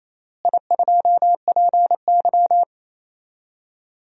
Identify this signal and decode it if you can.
Morse code, heard as a keyed tone, about 28 wpm, 700 Hz. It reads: I2PY
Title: DXCC-100 Call Signs - 4 Characters - Repeated 3x - 28wpm